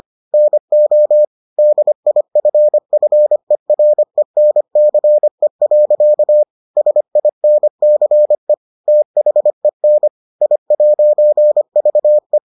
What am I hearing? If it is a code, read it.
Morse code, heard as a keyed tone, about 25 wpm, 600 Hz. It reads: NO DIFFERENCE. SINCE THEN I'VE